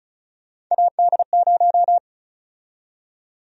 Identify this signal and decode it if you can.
Morse code, heard as a keyed tone, about 36 wpm, 700 Hz. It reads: AD0
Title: USA Call Sign Prefixes - Repeated 2x - 35wpm